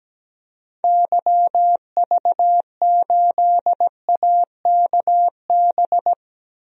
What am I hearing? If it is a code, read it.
Morse code, heard as a keyed tone, about 17 wpm, 700 Hz. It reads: YV8AKB
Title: DXCC-100 Call Signs - 6 Characters - Repeated 3x - 17wpm